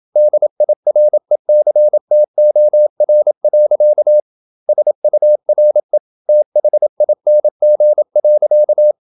Morse code, heard as a keyed tone, about 27 wpm, 600 Hz. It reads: DIRECTOR. SURE THING.
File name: Never 0027wpm